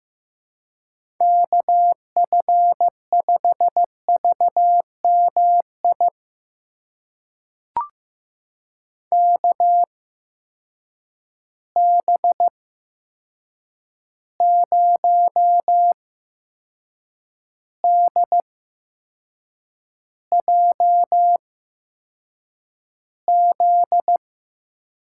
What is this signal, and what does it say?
Morse code, heard as a keyed tone, about 15 wpm, spaced as for 6 wpm, 700 Hz. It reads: KF5VMI KB0DJZ
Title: USA Call Signs - 6 Characters - 8x Character Spacing - 15wpm